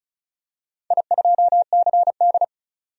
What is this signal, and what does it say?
Morse code, heard as a keyed tone, about 35 wpm, 700 Hz. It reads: I2CD
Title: DXCC-100 Call Signs - 4 Characters - 35wpm